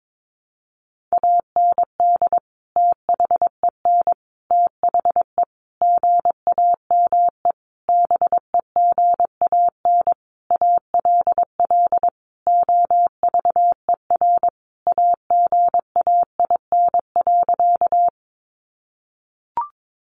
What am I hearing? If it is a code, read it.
Morse code, heard as a keyed tone, about 22 wpm, 700 Hz. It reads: AND THEN THE GAME BEGAN ALL OVER AGAIN.
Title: Sentences from Top 500 Words 22wpm